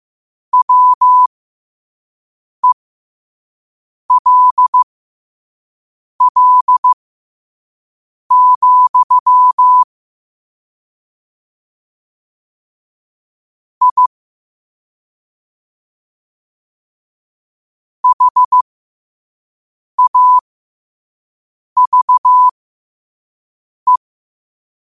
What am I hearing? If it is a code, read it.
Morse code, heard as a keyed tone, about 15 wpm, spaced as for 5 wpm, 1000 Hz. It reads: WELL, I HAVE
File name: qotdfiles / 5wpm